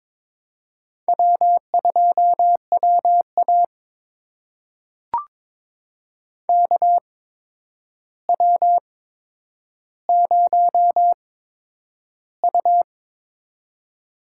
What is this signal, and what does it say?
Morse code, heard as a keyed tone, about 21 wpm, spaced as for 9 wpm, 700 Hz. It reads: W2WA KW0U
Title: USA Call Signs - 4 Characters - 8x Character Spacing - 22wpm